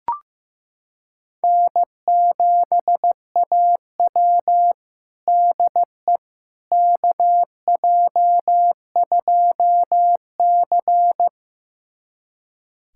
Morse code, heard as a keyed tone, about 15 wpm, 700 Hz. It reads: N7AW DE KJ2C
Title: USA Callsign DE Callsign - 4 Characters - 15wpm